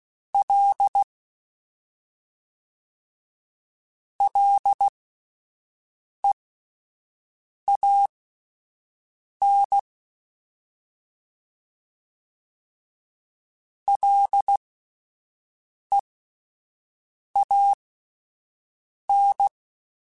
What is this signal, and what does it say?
Morse code, heard as a keyed tone, about 16 wpm, spaced as for 5 wpm, 800 Hz. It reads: L LEAN LEAN